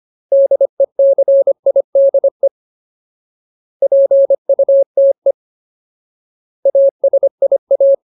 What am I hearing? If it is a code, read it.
Morse code, heard as a keyed tone, about 25 wpm, 550 Hz. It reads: DECIDE PUTE ASIA
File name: Ultimate Word List / 25WPM Ultimate